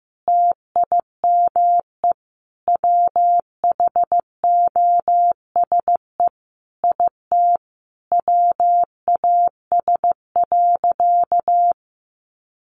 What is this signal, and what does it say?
Morse code, heard as a keyed tone, about 15 wpm, 700 Hz. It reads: TIME WHOSE IT WAS.
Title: Sentences from Top 1200 Words 15wpm